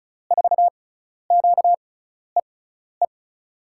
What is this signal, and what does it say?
Morse code, heard as a keyed tone, about 36 wpm, spaced as for 12 wpm, 700 Hz. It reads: 4QEE
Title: USA Call Signs - 6 Characters - 6x Character Spacing - 35wpm